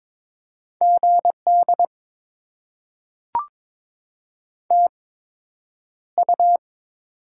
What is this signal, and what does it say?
Morse code, heard as a keyed tone, about 22 wpm, spaced as for 9 wpm, 700 Hz. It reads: GD TU
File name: Top 90 QSO Elements - 8x Character Spacing - 22wpm